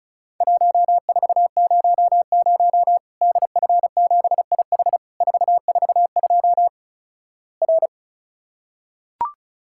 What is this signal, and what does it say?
Morse code, heard as a keyed tone, about 36 wpm, 700 Hz. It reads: 1400 DF7IH 442 R
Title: Worked All Europe DX Contest - QTC Exchange - 35wpm